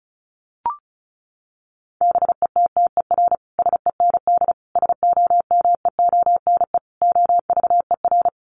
Morse code, heard as a keyed tone, about 34 wpm, 700 Hz. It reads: BETTER SEND SOMEONE OVER
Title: Sentences from Top 1500 Words 35wpm